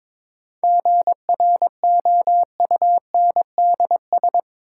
Morse code, heard as a keyed tone, about 22 wpm, 700 Hz. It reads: GROUNDS
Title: Sentences from Top 1300 Words - New Word Drill Alt - 22wpm